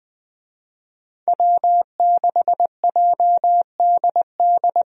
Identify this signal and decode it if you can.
Morse code, heard as a keyed tone, about 20 wpm, 700 Hz. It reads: W6JDD
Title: USA Call Signs - 5 Characters - Repeated 2x - 20wpm